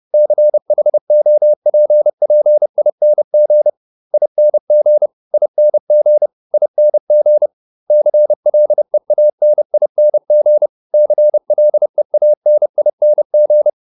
Morse code, heard as a keyed tone, about 30 wpm, 600 Hz. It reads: CHOPPING ING ING ING CLEANING CLEANING